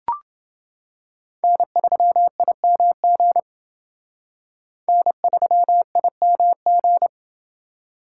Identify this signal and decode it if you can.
Morse code, heard as a keyed tone, about 28 wpm, 700 Hz. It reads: N3IMG N3IMG